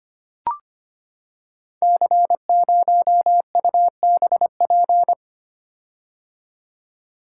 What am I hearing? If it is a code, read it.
Morse code, heard as a keyed tone, about 25 wpm, 700 Hz. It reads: C0UBP
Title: Code Groups - 5 Letter:Numbers - 25wpm